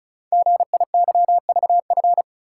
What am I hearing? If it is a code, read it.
Morse code, heard as a keyed tone, about 36 wpm, 700 Hz. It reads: GIYVF
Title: Code Groups - 5 Letters - 35wpm